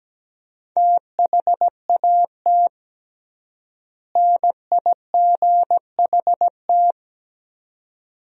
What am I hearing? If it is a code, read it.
Morse code, heard as a keyed tone, about 17 wpm, 700 Hz. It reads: THAT NIGHT
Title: 3x Word-Spacing - Sentences from Top 300 Words - 17wpm